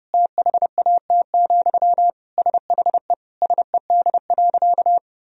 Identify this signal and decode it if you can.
Morse code, heard as a keyed tone, about 32 wpm, 700 Hz. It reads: THAT, SHE SED.
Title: Sentences from Top 200 Words and Abbreviations 30wpm